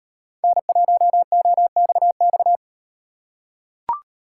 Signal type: Morse code

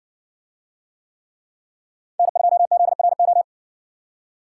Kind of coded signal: Morse code

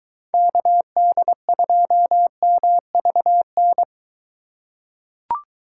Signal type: Morse code

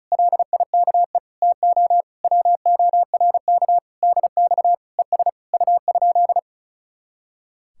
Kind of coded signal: Morse code

